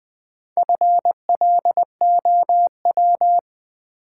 Morse code, heard as a keyed tone, about 20 wpm, 700 Hz. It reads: FLOW
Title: Sentences from Top 1200 Words - New Word Drill - 20wpm